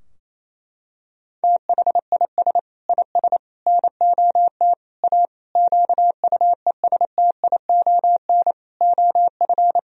Morse code, heard as a keyed tone, about 28 wpm, 700 Hz. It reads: THIS IS NOT A QUESTION OF
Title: Sentences from Top 1000 Words - Review Part 2 - 28wpm